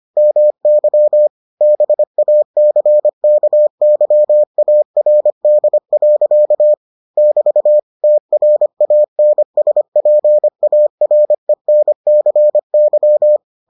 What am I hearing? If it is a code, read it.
Morse code, heard as a keyed tone, about 25 wpm, 600 Hz. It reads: MY BACKYARD. <BT> TRANSPARENCY